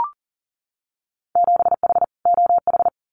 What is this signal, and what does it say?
Morse code, heard as a keyed tone, about 40 wpm, 700 Hz. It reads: 7H OH
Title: Winter Field Day Exchanges Encore 40wpm